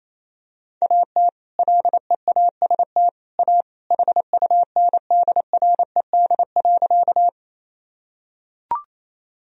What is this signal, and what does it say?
Morse code, heard as a keyed tone, about 28 wpm, 700 Hz. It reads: AT LEAST A HUNDRED.
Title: Sentences from Top 1000 Words - Review Part 2 - 28wpm